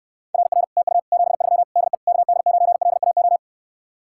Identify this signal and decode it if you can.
Morse code, heard as a keyed tone, about 69 wpm, 700 Hz. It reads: RR TU 73 DE KD9LAO